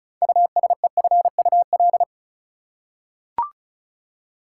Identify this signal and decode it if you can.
Morse code, heard as a keyed tone, about 34 wpm, 700 Hz. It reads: USEFUL
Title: Six Letter Words - 35wpm